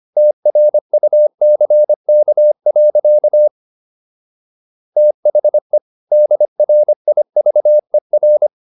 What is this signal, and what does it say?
Morse code, heard as a keyed tone, about 25 wpm, 600 Hz. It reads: TRUCK. THE DRIVER